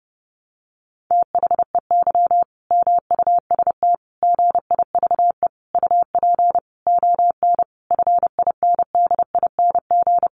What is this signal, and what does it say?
Morse code, heard as a keyed tone, about 30 wpm, 700 Hz. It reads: THEY MUST GIVE UP ON FINDING
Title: Sentences from Top 1100 Words 30wpm